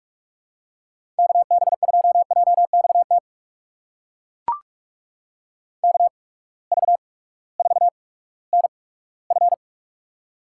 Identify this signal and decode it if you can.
Morse code, heard as a keyed tone, about 41 wpm, spaced as for 20 wpm, 700 Hz. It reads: KB2JXT KV4NF
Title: USA Call Signs - 8x Character Spacing - 45wpm